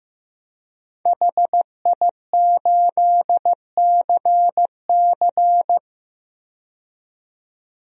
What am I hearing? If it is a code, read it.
Morse code, heard as a keyed tone, about 15 wpm, 700 Hz. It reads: HI8CC
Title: DXCC-100 Call Signs - 5 Characters - 15wpm